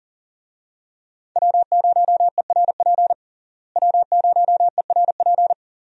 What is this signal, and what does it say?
Morse code, heard as a keyed tone, about 40 wpm, 700 Hz. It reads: W0ERP W0ERP
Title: USA Call Signs - 5 Characters - Repeated 2x - 40wpm